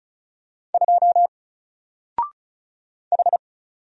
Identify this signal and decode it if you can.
Morse code, heard as a keyed tone, about 34 wpm, 700 Hz. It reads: 2 H
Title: Single Character (Letter:Number:Punctuation:Prosign) Alt 35wpm